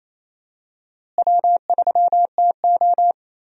Morse code, heard as a keyed tone, about 28 wpm, 700 Hz. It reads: W3TO
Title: USA Call Signs - 4 Characters - Repeated 3x - 28wpm